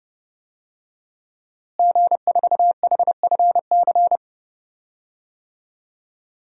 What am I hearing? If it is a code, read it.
Morse code, heard as a keyed tone, about 32 wpm, 700 Hz. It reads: G4HFC
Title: DXCC-100 Call Signs - 5 Characters - Repeated 3x - 30wpm